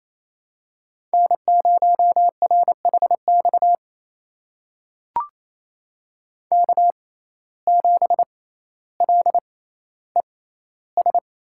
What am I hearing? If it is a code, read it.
Morse code, heard as a keyed tone, about 27 wpm, spaced as for 14 wpm, 700 Hz. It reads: N0RHX K7LES
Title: USA Call Signs - 5 Characters - 6x Character Spacing - 28wpm